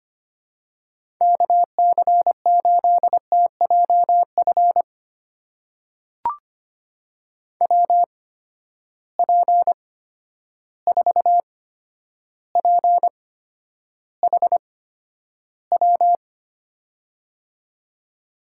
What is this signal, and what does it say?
Morse code, heard as a keyed tone, about 25 wpm, spaced as for 11 wpm, 700 Hz. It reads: KC8TJF WP4PHW
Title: USA Call Signs - 6 Characters - 8x Character Spacing - 25wpm